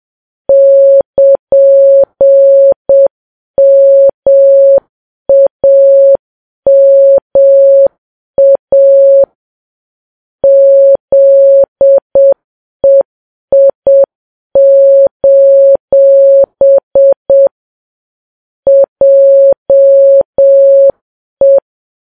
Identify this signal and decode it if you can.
Morse code, heard as a keyed tone, about 7 wpm, 550 Hz. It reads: <KN>MAMA ZEI: JE
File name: Files Web tls / Morse on9fv / morse7kps